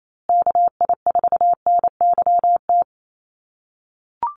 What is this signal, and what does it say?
Morse code, heard as a keyed tone, about 28 wpm, 700 Hz. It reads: KI4NYT
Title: USA Call Signs Repeated 2x 28wpm